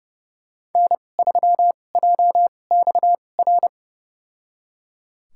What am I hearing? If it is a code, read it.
Morse code, heard as a keyed tone, about 32 wpm, spaced as for 22 wpm, 700 Hz. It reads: N3JXR
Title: USA Call Signs - 5 Characters - 2x Character Spacing - 30wpm